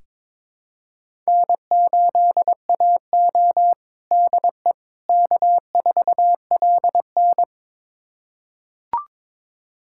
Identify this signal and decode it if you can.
Morse code, heard as a keyed tone, about 22 wpm, 700 Hz. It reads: N8AO DE K4LN
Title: USA Callsign DE Callsign - 4 Characters - 22wpm